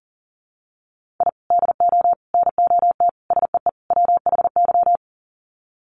Morse code, heard as a keyed tone, about 37 wpm, 700 Hz. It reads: I DO NOT SEE WHY
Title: Sets of 5 Words Encore 40wpm